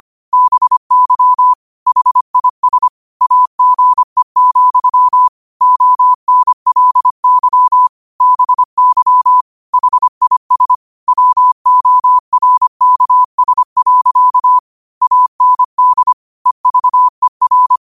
Morse code, heard as a keyed tone, about 25 wpm, 1000 Hz. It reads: BY HIS AGE, ONLY BY HIS WORKS. AND EVER